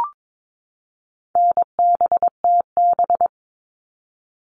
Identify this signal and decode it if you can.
Morse code, heard as a keyed tone, about 22 wpm, 700 Hz. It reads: NBTB
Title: All Cut Numbers - 4-Digit 22wpm